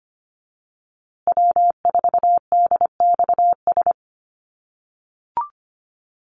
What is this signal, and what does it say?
Morse code, heard as a keyed tone, about 25 wpm, 700 Hz. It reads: W4DXS
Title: USA Call Signs Repeated 2x 25wpm